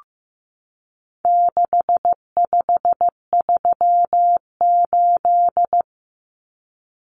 Morse code, heard as a keyed tone, about 15 wpm, 700 Hz. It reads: 6538
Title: Common Cut Numbers - 4-Digit 15wpm